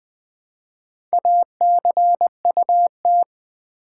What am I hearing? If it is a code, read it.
Morse code, heard as a keyed tone, about 20 wpm, 700 Hz. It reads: ACUT
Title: Build A Word - 1 Letter At A Time - 3 to 5-Letter Words - 20wpm